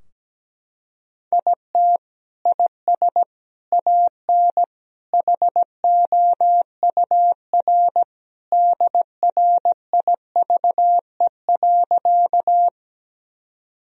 Morse code, heard as a keyed tone, about 17 wpm, 700 Hz. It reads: IT IS AN HOUR DRIVE.
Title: Sentences from Top 1000 Words - Review Part 2 - 17wpm